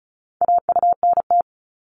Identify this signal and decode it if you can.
Morse code, heard as a keyed tone, about 36 wpm, 700 Hz. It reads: AUNT
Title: Multi-letter Phonemes 35wpm